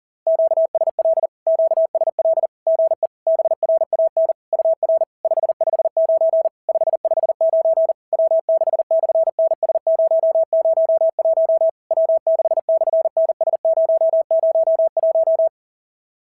Morse code, heard as a keyed tone, about 40 wpm, 660 Hz. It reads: QSL QSL GE BRAN UR 559 559 W6/NS001 W6/NS001
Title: SOTA QSO Practice 40wpm